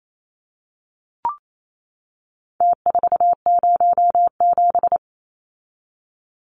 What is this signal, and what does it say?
Morse code, heard as a keyed tone, about 28 wpm, 700 Hz. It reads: T407
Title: Numbers - Contest Style - 4-Digit 28wpm